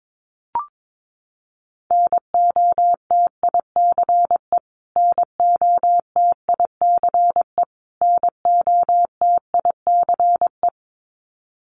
Morse code, heard as a keyed tone, about 22 wpm, 700 Hz. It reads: NOTICE NOTICE NOTICE
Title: Top 500 Words Repeated 22wpm